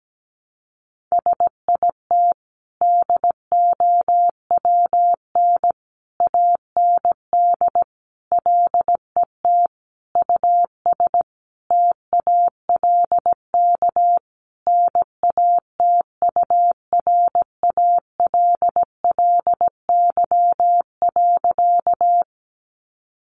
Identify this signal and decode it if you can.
Morse code, heard as a keyed tone, about 17 wpm, 700 Hz. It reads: SIT DOWN AND LET US TALK NATURALLY.
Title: Sentences from Top 1300 Words 17wpm